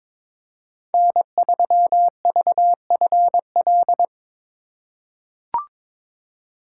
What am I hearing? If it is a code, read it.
Morse code, heard as a keyed tone, about 22 wpm, 700 Hz. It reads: N3VFL
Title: USA Call Signs - 5 Characters - 4x Character Spacing - 22wpm